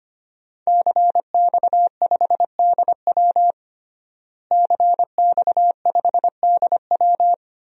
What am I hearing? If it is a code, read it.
Morse code, heard as a keyed tone, about 25 wpm, 700 Hz. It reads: CX5DW CX5DW
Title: DXCC-100 Call Signs - 5 Characters - Repeated 3x - 25wpm